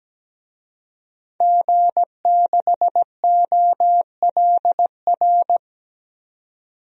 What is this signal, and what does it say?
Morse code, heard as a keyed tone, about 17 wpm, 700 Hz. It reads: G6OLR
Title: DXCC-100 Call Signs - 5 Characters - 17wpm